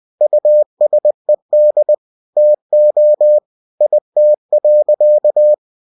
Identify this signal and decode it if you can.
Morse code, heard as a keyed tone, about 20 wpm, 600 Hz. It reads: USED TO IT.
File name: cw-books / winnie-the-pooh / winnie-20wpm-page-0074